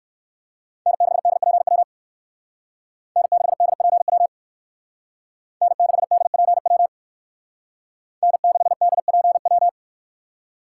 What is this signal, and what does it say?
Morse code, heard as a keyed tone, about 59 wpm, 700 Hz. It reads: N6DPW N6DPW N6DPW N6DPW